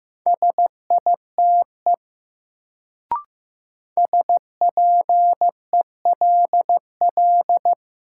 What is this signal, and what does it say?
Morse code, heard as a keyed tone, about 15 wpm, 700 Hz. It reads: SITE SPELL
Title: Sentences from Top 1000 Words - Review Part 1 - New Word Drill - 15wpm